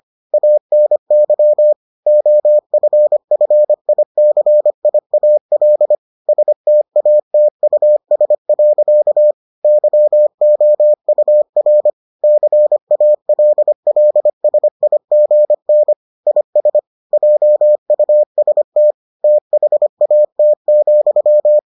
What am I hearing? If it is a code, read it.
Morse code, heard as a keyed tone, about 25 wpm, 600 Hz. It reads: ANY OFFICIAL STATUS. YOUR CALLSIGN IS JUST THAT,